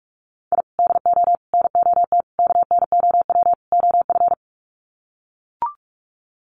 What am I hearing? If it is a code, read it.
Morse code, heard as a keyed tone, about 44 wpm, 690 Hz. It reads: I DO NOT KNOW OF
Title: Sets of 5 Words Encore2 45wpm